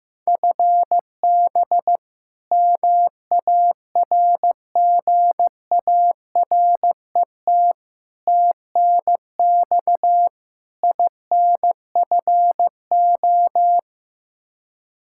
Morse code, heard as a keyed tone, about 15 wpm, 700 Hz. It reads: FB MARGARET TNX INFO
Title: Practice QSO - Exchange 3 - More Info - 15wpm